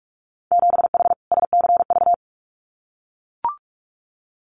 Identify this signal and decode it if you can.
Morse code, heard as a keyed tone, about 44 wpm, 700 Hz. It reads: 7H SCV
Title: Winter Field Day Exchanges Encore 45wpm